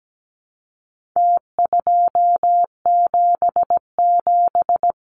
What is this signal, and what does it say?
Morse code, heard as a keyed tone, about 17 wpm, 700 Hz. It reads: T277
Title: Numbers - Contest Style - 4-Digit 17wpm